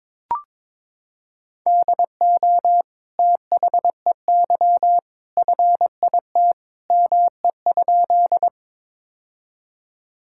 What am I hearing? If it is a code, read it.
Morse code, heard as a keyed tone, about 22 wpm, 700 Hz. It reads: DO THEY FIT ME?
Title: Sentences from Top 1000 Words - Review Part 5 - 22wpm